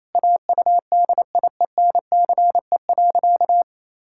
Morse code, heard as a keyed tone, about 28 wpm, 700 Hz. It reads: AUDIENCE.
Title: Sentences from Top 1000 Words - Review Part 5 - 28wpm